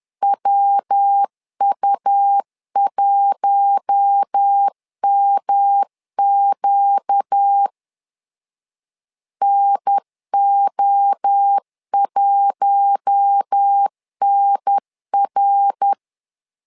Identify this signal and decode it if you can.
Morse code, heard as a keyed tone, about 10 wpm, 800 Hz. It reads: WU1MQ NO1NR
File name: LEZIONI / 4 Q1 10wpm 2448